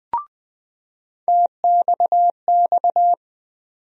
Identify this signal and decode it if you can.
Morse code, heard as a keyed tone, about 20 wpm, 700 Hz. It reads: TXX
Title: USA Call Sign Postfixes - 3 Letters 20wpm